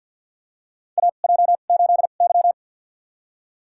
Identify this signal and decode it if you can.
Morse code, heard as a keyed tone, about 53 wpm, spaced as for 37 wpm, 700 Hz. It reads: AJ8Y
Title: USA Call Signs - 2x Character Spacing - 50wpm